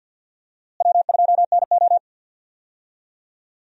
Morse code, heard as a keyed tone, about 53 wpm, 700 Hz. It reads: W2NO
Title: USA Call Signs - 4 Characters - Repeated 2x - 50wpm